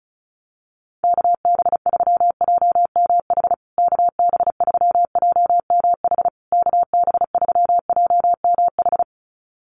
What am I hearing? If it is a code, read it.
Morse code, heard as a keyed tone, about 34 wpm, 700 Hz. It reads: KB3JMH KB3JMH KB3JMH
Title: USA Call Signs Repeated 3x 35wpm